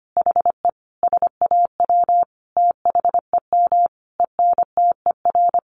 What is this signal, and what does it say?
Morse code, heard as a keyed tone, about 25 wpm, 700 Hz. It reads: HE SAW THEM ENTER
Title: Sentences from Top 1300 Words 25wpm